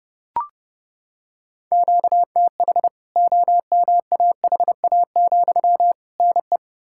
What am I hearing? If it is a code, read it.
Morse code, heard as a keyed tone, about 30 wpm, 700 Hz. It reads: QTH OMAHA, NE
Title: K1USN SST QSO Practice 30wpm